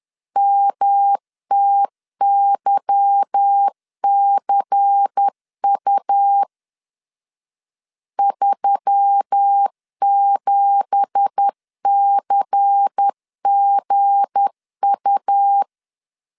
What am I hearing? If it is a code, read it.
Morse code, heard as a keyed tone, about 10 wpm, 800 Hz. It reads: MTYCU 37CGU
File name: LEZIONI / 10 SIGNS / 10A SLASH 10wpm 2808